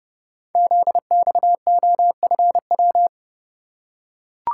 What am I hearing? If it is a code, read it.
Morse code, heard as a keyed tone, about 29 wpm, 700 Hz. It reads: ZXOFW
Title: Code Groups - 5 Letter:Numbers - 30wpm